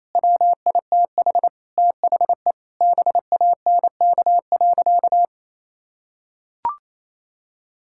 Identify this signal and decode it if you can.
Morse code, heard as a keyed tone, about 28 wpm, 700 Hz. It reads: WITH THE BANK.
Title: Sentences from Top 1000 Words - Review Part 6 - 28wpm